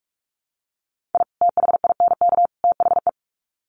Morse code, heard as a keyed tone, about 44 wpm, 700 Hz. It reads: I THINK THE
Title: Sets of 3 Words 45wpm